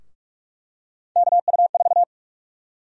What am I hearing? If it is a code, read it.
Morse code, heard as a keyed tone, about 48 wpm, 700 Hz. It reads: KU4
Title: USA Call Sign Prefixes - 45wpm